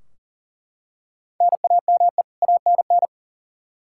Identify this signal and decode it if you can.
Morse code, heard as a keyed tone, about 40 wpm, 700 Hz. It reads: NAME ANN
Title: K1USN SST QSO Practice 40wpm